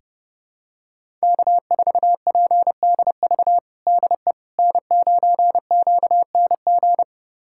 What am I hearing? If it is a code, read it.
Morse code, heard as a keyed tone, about 32 wpm, 700 Hz. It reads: K4PDV DE N9QNG
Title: USA Callsign DE Callsign 30wpm